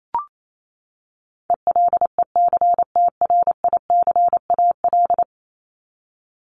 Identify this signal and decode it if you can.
Morse code, heard as a keyed tone, about 28 wpm, 700 Hz. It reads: ELECTRICAL
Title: Top 751-1000 Words 28wpm